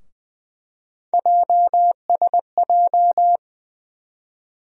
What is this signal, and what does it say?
Morse code, heard as a keyed tone, about 20 wpm, 700 Hz. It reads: JSJ
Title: USA Call Sign Postfixes - 3 Letters 20wpm